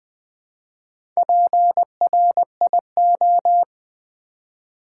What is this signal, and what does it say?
Morse code, heard as a keyed tone, about 20 wpm, 700 Hz. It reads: PRIO
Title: Build A Word - 1 Letter At A Time - 5-Letter Words - 20wpm